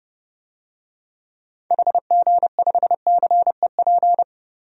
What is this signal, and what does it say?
Morse code, heard as a keyed tone, about 30 wpm, 700 Hz. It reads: HG5CEP
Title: DXCC-100 Call Signs - 6 Characters - 30wpm